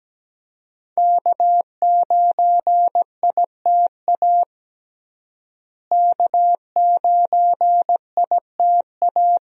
Morse code, heard as a keyed tone, about 17 wpm, 700 Hz. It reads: K9ITA K9ITA
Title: USA Call Signs - 5 Characters - Repeated 2x - 17wpm